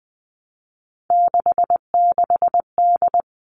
Morse code, heard as a keyed tone, about 20 wpm, 700 Hz. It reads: 66D
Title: All Cut Numbers - 3-Digit 20wpm